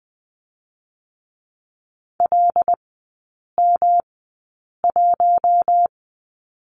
Morse code, heard as a keyed tone, about 20 wpm, spaced as for 8 wpm, 700 Hz. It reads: LM1
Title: 107 Lesson 18e - Letter m - Three Characters 20wpm